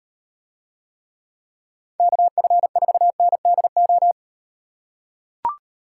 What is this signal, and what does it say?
Morse code, heard as a keyed tone, about 36 wpm, 700 Hz. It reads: KF4NDO